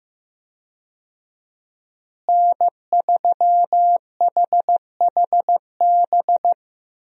Morse code, heard as a keyed tone, about 15 wpm, 700 Hz. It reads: N3HHB
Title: USA Call Signs - 5 Characters - Repeated 2x - 15wpm